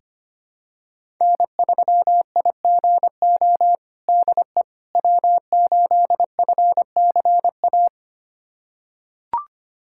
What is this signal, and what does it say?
Morse code, heard as a keyed tone, about 25 wpm, 700 Hz. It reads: N3IGO DE W8FCA
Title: USA Callsign DE Callsign - 5 Characters - 25wpm